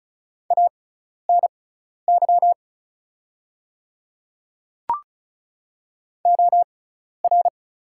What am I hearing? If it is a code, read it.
Morse code, heard as a keyed tone, about 36 wpm, spaced as for 12 wpm, 690 Hz. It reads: ANY OR
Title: 6x Character Spacing - Top 200 Words - 35wpm